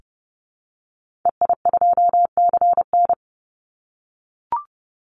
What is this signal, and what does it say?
Morse code, heard as a keyed tone, about 30 wpm, 700 Hz. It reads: EI2CN
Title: Call Signs - 30wpm